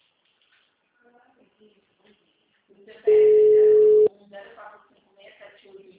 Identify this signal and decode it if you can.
Ringback tone (Europe, 425 Hz), with speech.